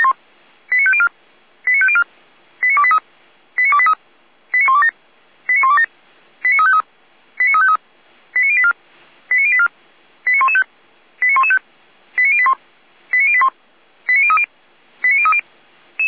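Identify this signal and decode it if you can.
Five-tone selective call (ZVEI), heard as quick runs of five tones, tones of about 70 ms. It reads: 89504 89504 89283 89283 89127 89127 89353 89353 89084 89084 89115 89115 89081 89081 89030 89030
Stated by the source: ZVEI five-tone selective call 89143 received with an SDR.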